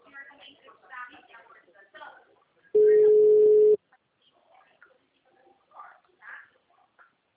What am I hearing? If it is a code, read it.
Ringback tone (Europe, 425 Hz), with speech.